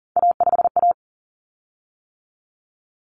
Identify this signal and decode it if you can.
Morse code, heard as a keyed tone, about 40 wpm, 700 Hz. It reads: A5A
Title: Common Cut Numbers - 3-Digit 40wpm